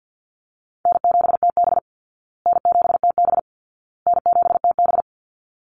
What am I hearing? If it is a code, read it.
Morse code, heard as a keyed tone, about 52 wpm, 700 Hz. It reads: N7TB N7TB N7TB
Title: Call Signs Repeated 2x - 50wpm